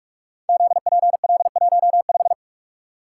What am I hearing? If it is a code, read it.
Morse code, heard as a keyed tone, about 44 wpm, 700 Hz. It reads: ZPL15
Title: Code Groups - 5 Letter:Numbers - 45wpm